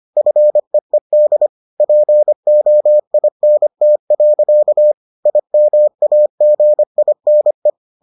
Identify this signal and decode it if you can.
Morse code, heard as a keyed tone, about 25 wpm, 600 Hz. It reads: FEED POINT. IMAGINE